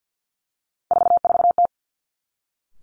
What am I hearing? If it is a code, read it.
Morse code, heard as a keyed tone, about 50 wpm, 700 Hz. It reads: 44T
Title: Three Challenging Letter-Number Combinations 50wpm